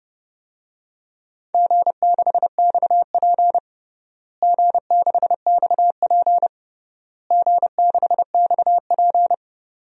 Morse code, heard as a keyed tone, about 32 wpm, 700 Hz. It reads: G6XP G6XP G6XP
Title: DXCC-100 Call Signs - 4 Characters - Repeated 3x - 30wpm